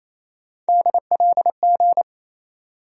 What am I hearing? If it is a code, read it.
Morse code, heard as a keyed tone, about 27 wpm, 700 Hz. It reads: DLG
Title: USA Call Sign Postfixes - 3 Letters 28wpm